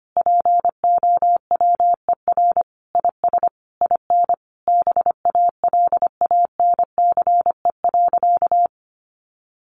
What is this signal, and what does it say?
Morse code, heard as a keyed tone, about 25 wpm, 700 Hz. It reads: POWER IS IN BALANCE.
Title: Sentences from Top 900 Words 25wpm